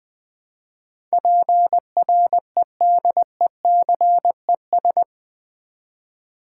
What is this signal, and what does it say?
Morse code, heard as a keyed tone, about 20 wpm, 700 Hz. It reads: PREDECES
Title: Build A Word - 4 Letters At A Time - 11-Letter Words - 20wpm